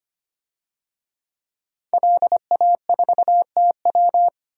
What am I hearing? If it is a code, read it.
Morse code, heard as a keyed tone, about 25 wpm, 700 Hz. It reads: LA4TW
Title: DXCC-100 Call Signs - 5 Characters - 25wpm